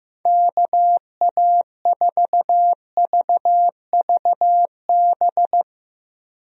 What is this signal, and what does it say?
Morse code, heard as a keyed tone, about 15 wpm, 700 Hz. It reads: KA4VVB
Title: USA Call Signs - 6 Characters - Repeated 2x - 15wpm